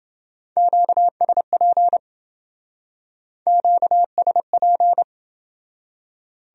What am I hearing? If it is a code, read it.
Morse code, heard as a keyed tone, about 29 wpm, 700 Hz. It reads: QSP QSP